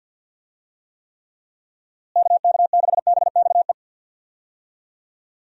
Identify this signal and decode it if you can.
Morse code, heard as a keyed tone, about 47 wpm, 700 Hz. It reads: KK6BXE
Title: USA Call Signs - 6 Characters - Repeated 3x - 50wpm